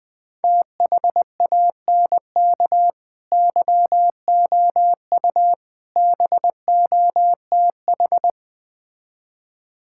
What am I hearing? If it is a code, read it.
Morse code, heard as a keyed tone, about 20 wpm, 700 Hz. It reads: THANK YOU BOTH
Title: Build A Phrase - 5-Word Phrases - 20wpm